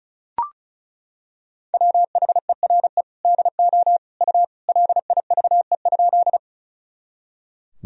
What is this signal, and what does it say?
Morse code, heard as a keyed tone, about 36 wpm, 690 Hz. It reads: WHERE DO U LIVE?
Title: Sentences from Top 200 Words and Abbreviations 35wpm